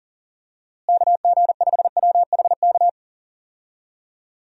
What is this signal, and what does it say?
Morse code, heard as a keyed tone, about 37 wpm, 700 Hz. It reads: KG5WHK
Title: USA Call Signs - 6 Characters - Repeated 2x - 40wpm